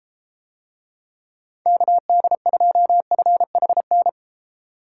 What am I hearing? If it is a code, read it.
Morse code, heard as a keyed tone, about 33 wpm, 700 Hz. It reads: KD2FHN